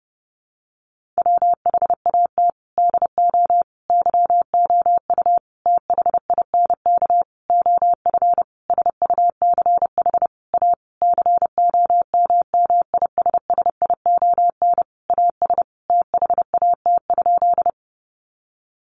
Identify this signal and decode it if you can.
Morse code, heard as a keyed tone, about 30 wpm, 700 Hz. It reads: WHAT DO YOU THINK OF SUCH A COMMISSION AS THAT?
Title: Sentences from Top 1400 Words 30wpm